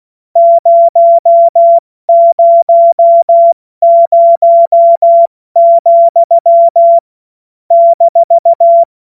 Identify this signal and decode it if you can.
Morse code, heard as a keyed tone, about 16 wpm, spaced as for 12 wpm, 680 Hz. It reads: 000,-
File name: Morsekurs / Witze / Witz 06. September 2022 70 F80